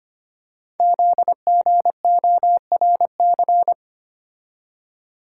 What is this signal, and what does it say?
Morse code, heard as a keyed tone, about 25 wpm, 700 Hz. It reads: ZGORC
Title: Code Groups - 5 Letters - 25wpm